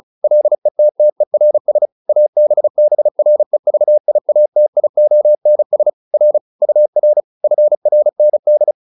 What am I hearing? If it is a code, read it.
Morse code, heard as a keyed tone, about 36 wpm, 600 Hz. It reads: LETTERS ABBREVIATIONS R UR FRND